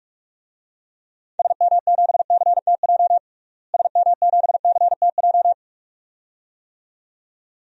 Morse code, heard as a keyed tone, about 46 wpm, 700 Hz. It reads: SM7CTJ SM7CTJ
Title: DXCC-100 Call Signs - 6 Characters - Repeated 2x - 45wpm